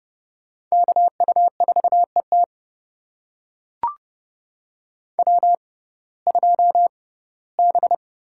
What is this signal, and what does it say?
Morse code, heard as a keyed tone, about 28 wpm, spaced as for 18 wpm, 700 Hz. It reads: KU4ET W2B
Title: USA Call Signs - 5 Characters - 6x Character Spacing - 30wpm